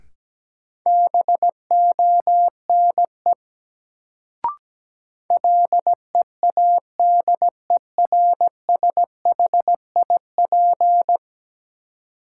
Morse code, heard as a keyed tone, about 17 wpm, 700 Hz. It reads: BONE LEADERSHIP
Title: Sentences from Top 1600 Words - New Word Drill Alt - 17wpm